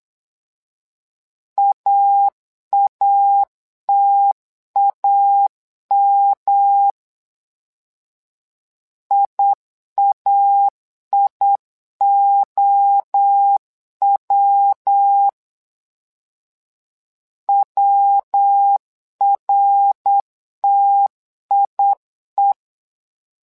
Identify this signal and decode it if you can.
Morse code, heard as a keyed tone, about 8 wpm, 800 Hz. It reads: AATAM IAIOW WRTIE
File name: LEZIONI / 3 MWRO / 3 MWRO 8wpm 2606